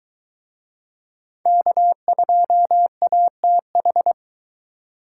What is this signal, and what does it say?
Morse code, heard as a keyed tone, about 24 wpm, 700 Hz. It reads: K2ATH